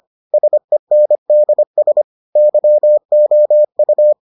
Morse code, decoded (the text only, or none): SENDS YOU